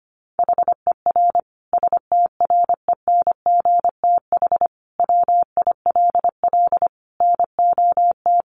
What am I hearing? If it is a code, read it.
Morse code, heard as a keyed tone, about 25 wpm, 700 Hz. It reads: HER STRENGTH WILL NOT
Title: Sentences from Top 1300 Words 25wpm